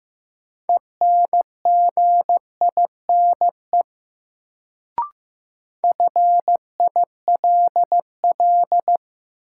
Morse code, heard as a keyed tone, about 15 wpm, 700 Hz. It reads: ENGINE FILL
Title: Sentences from Top 1500 Words - New Word Drill Alt - 15wpm